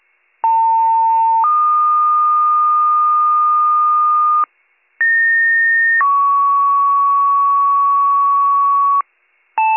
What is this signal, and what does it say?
Two-tone page, heard as a one-second tone, then a three-second tone, about 900 then 1220 Hz, about 1850 then 1120 Hz. It is heard from 0.4 to 9.0 s.